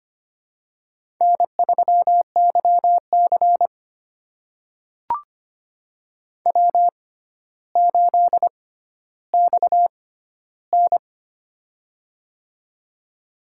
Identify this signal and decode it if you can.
Morse code, heard as a keyed tone, about 25 wpm, spaced as for 13 wpm, 700 Hz. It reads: N3YC W8XN
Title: USA Call Signs - 4 Characters - 6x Character Spacing - 25wpm